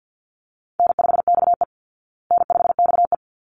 Morse code, heard as a keyed tone, about 50 wpm, 700 Hz. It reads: N5XE N5XE
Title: Easy Call Signs Repeated 2x - 50wpm